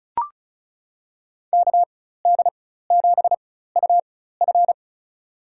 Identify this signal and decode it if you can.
Morse code, heard as a keyed tone, about 36 wpm, spaced as for 16 wpm, 700 Hz. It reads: KD7UF
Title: USA Call Signs - 5 Characters - 4x Character Spacing - 35wpm